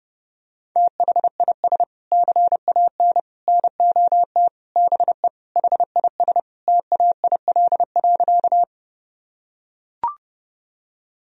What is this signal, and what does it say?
Morse code, heard as a keyed tone, about 30 wpm, 700 Hz. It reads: THIS CAN NOT BE HIS TAIL.
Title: Sentences from Top 1000 Words - Review Part 3 - 30wpm